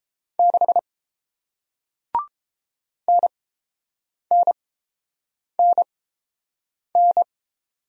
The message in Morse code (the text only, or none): 6 NNNN